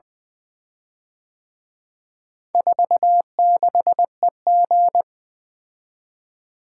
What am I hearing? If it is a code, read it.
Morse code, heard as a keyed tone, about 24 wpm, 700 Hz. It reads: 46EG
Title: Speed-Racing - Four Letter-Number Combinations - 20wpm